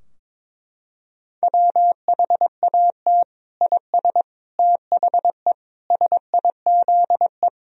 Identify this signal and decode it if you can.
Morse code, heard as a keyed tone, about 22 wpm, 700 Hz. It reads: WHAT IS THE SIZE
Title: Sentences from Top 1000 Words - Review Part 2 - 22wpm